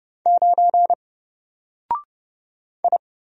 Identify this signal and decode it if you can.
Morse code, heard as a keyed tone, about 30 wpm, 700 Hz. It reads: 9 I
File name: Single-Letter-Number-30wpm